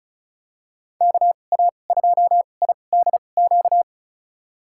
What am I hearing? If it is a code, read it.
Morse code, heard as a keyed tone, about 36 wpm, spaced as for 25 wpm, 700 Hz. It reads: KA2IDQ
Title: USA Call Signs - 6 Characters - 2x Character Spacing - 35wpm